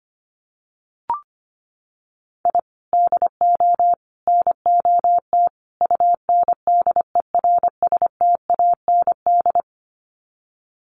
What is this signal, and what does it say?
Morse code, heard as a keyed tone, about 25 wpm, 700 Hz. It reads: I DO NOT UNDERSTAND
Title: Sets of 4 Words 25wpm